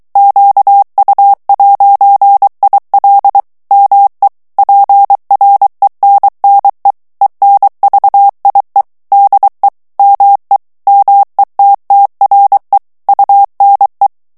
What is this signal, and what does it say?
Morse code, heard as a keyed tone, about 23 wpm, 800 Hz. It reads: QU'IL ME PRENNE ENVIE DE ME METTRE UNE
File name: CW / FICHIERS-MP3 / FRANCAIS TEXTE CLAIR 2 VITESSE 22 MOTS MINUTES